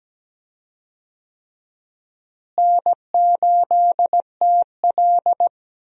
Morse code, heard as a keyed tone, about 17 wpm, 700 Hz. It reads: N8TL
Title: USA Call Signs - 4 Characters - Repeated 3x - 17wpm